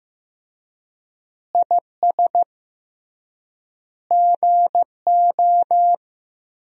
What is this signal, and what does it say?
Morse code, heard as a keyed tone, about 15 wpm, 700 Hz. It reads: IS GO
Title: 3x Word-Spacing - Sentences from Top 200 Words - 15wpm